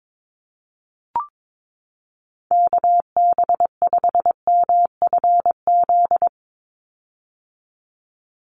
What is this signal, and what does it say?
Morse code, heard as a keyed tone, about 22 wpm, 700 Hz. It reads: KB5MFZ
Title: USA Call Signs 22wpm